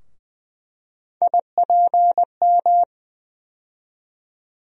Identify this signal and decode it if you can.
Morse code, heard as a keyed tone, about 20 wpm, 700 Hz. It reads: IPM
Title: USA Call Sign Postfixes - 3 Letters 20wpm